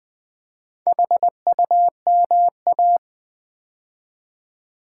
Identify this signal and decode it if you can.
Morse code, heard as a keyed tone, about 20 wpm, 700 Hz. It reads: HUMA
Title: Build A Word - 2 Letters At A Time - 8-Letter Words - 20wpm